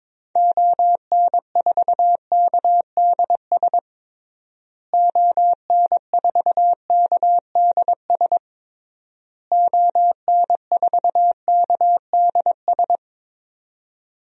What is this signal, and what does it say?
Morse code, heard as a keyed tone, about 21 wpm, 700 Hz. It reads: ON4KDS ON4KDS ON4KDS
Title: DXCC-100 Call Signs - 6 Characters - Repeated 3x - 22wpm